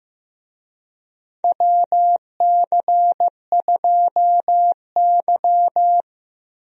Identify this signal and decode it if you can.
Morse code, heard as a keyed tone, about 15 wpm, 700 Hz. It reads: WC2Y
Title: USA Call Signs - 4 Characters - Repeated 2x - 15wpm